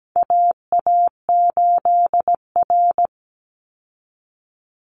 Morse code, heard as a keyed tone, about 17 wpm, 700 Hz. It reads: AA8R
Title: Call Signs Repeated 3x - 17wpm